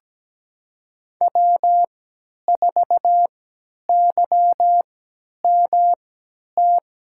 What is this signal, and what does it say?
Morse code, heard as a keyed tone, about 17 wpm, spaced as for 10 wpm, 700 Hz. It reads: W4YMT
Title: USA Call Signs - 5 Characters - 3x Character Spacing - 17wpm